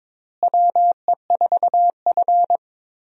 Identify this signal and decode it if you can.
Morse code, heard as a keyed tone, about 22 wpm, 700 Hz. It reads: WE4F
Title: USA Call Signs - 4 Characters - Repeated 3x - 22wpm